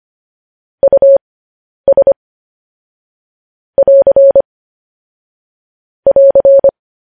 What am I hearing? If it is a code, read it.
Morse code, heard as a keyed tone, about 25 wpm, spaced as for 10 wpm, 550 Hz. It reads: US <AR> <AR>